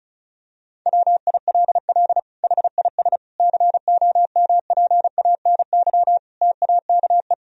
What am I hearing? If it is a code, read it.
Morse code, heard as a keyed tone, about 34 wpm, 700 Hz. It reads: WILL HIS COMPANY TAKE
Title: Sentences from Top 1000 Words - Review Part 4 - 35wpm